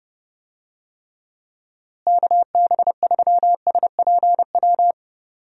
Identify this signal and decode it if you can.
Morse code, heard as a keyed tone, about 31 wpm, 700 Hz. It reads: KB3SPW
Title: USA Call Signs - 6 Characters - Repeated 3x - 30wpm